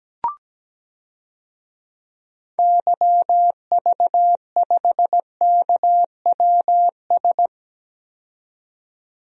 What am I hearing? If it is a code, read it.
Morse code, heard as a keyed tone, about 17 wpm, 700 Hz. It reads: YV5KWS
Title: DXCC-100 Call Signs - 6 Characters - 17wpm